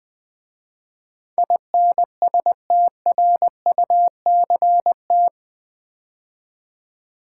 Morse code, heard as a keyed tone, about 20 wpm, 700 Hz. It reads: INSTRUCT
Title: 8 Letter Words - First 3 Letters Given - 20wpm